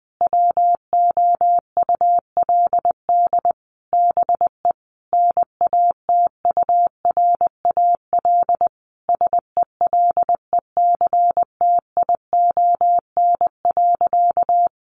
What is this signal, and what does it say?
Morse code, heard as a keyed tone, about 20 wpm, 700 Hz. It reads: WOULD BE NATURAL SELECTION.
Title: Sentences from Top 1400 Words 20wpm